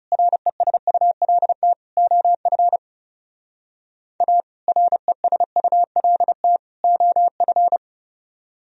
Morse code, heard as a keyed tone, about 37 wpm, 700 Hz. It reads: RESULT OF A RESULT OF
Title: Speed-Racing - Sets of 3 Words 30wpm